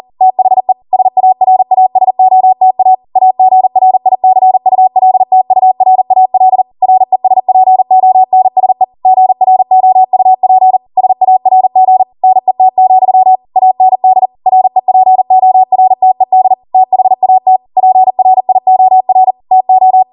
Morse code, heard as a keyed tone, about 40 wpm, 750 Hz. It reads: THE SARASOTA AGRICULTURAL RESPONSE GROUP SARG NET, AND REPORTED THAT PRIOR TO